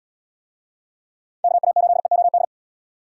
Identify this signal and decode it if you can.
Morse code, heard as a keyed tone, about 68 wpm, 700 Hz. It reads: KA8EOD